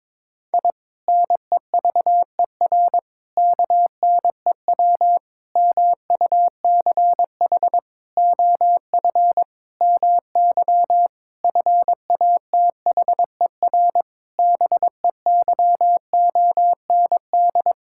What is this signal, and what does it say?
Morse code, heard as a keyed tone, about 22 wpm, 700 Hz. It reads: I NEVER KNEW MUCH OF MY FATHER BEYOND
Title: Sentences from Top 1000 Words - Review Part 6 - 22wpm